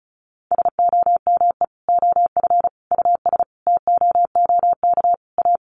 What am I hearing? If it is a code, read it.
Morse code, heard as a keyed tone, about 34 wpm, 700 Hz. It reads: SOME OF US TOOK A
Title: Sentences from Top 1400 Words 35wpm